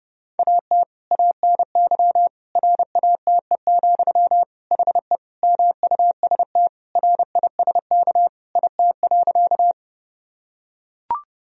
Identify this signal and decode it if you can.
Morse code, heard as a keyed tone, about 30 wpm, 700 Hz. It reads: AT ANY RATE, HE MUST RISK IT.
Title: Sentences from Top 1000 Words - Review Part 4 - 30wpm